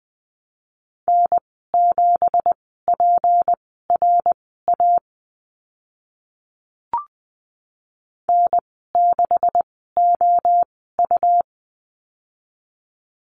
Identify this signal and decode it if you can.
Morse code, heard as a keyed tone, about 20 wpm, spaced as for 14 wpm, 700 Hz. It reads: N7PRA N6OU
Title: 179 Lesson 29g - Slash - Call Signs 20wpm